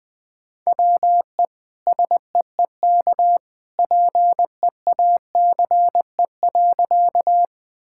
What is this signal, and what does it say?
Morse code, heard as a keyed tone, about 20 wpm, 700 Hz. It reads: WE SEEK PEACE.
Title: Sentences from Top 1600 Words 20wpm